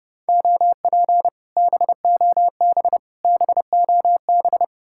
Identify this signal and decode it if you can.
Morse code, heard as a keyed tone, about 35 wpm, 700 Hz. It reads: OP BOB BOB
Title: Speed-Racing Common QSO Phrases - 30wpm